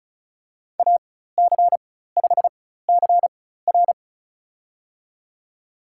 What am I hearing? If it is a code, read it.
Morse code, heard as a keyed tone, about 36 wpm, spaced as for 16 wpm, 700 Hz. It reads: AC5CR
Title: USA Call Signs - 5 Characters - 4x Character Spacing - 35wpm